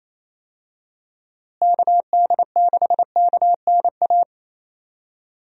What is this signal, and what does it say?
Morse code, heard as a keyed tone, about 28 wpm, 700 Hz. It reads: KD6KNA
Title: USA Call Signs - 6 Characters - Repeated 2x - 28wpm